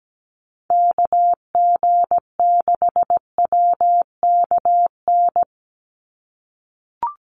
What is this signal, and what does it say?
Morse code, heard as a keyed tone, about 17 wpm, 700 Hz. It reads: KG6WKN
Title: USA Call Signs 17wpm